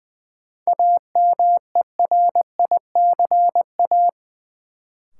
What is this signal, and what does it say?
Morse code, heard as a keyed tone, about 20 wpm, 700 Hz. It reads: AMERICA
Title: Sets of 5 Words - New Word Drill - 20wpm